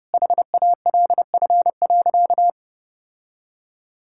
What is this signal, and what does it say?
Morse code, heard as a keyed tone, about 28 wpm, 700 Hz. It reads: HALF.
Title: Sentences from Top 1000 Words - Review Part 5 - 30wpm